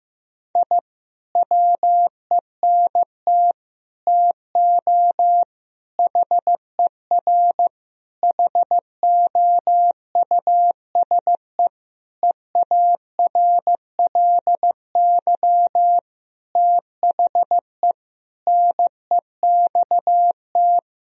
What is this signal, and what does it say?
Morse code, heard as a keyed tone, about 15 wpm, 700 Hz. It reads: I WENT TO HER HOUSE EARLY THE NEXT
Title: Sentences from Top 1000 Words - Review Part 2 - 15wpm